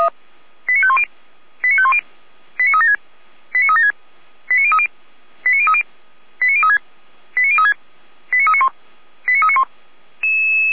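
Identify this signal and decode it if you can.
Five-tone selective call (ZVEI), heard as quick runs of five tones, tones of about 70 ms. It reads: 89510 89510 89376 89376 89030 89030 89036 89036 89391 89391
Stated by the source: ZVEI five-tone call received by SDR, from the server of Feuerwehr Orlamünde (fire brigade).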